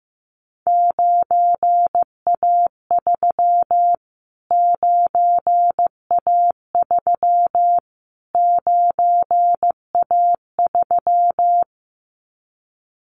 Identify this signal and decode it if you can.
Morse code, heard as a keyed tone, about 15 wpm, 700 Hz. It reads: 9A3 9A3 9A3
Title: Callsign Prefixes for Top 25 DXCC - Repeated 3x - 15wpm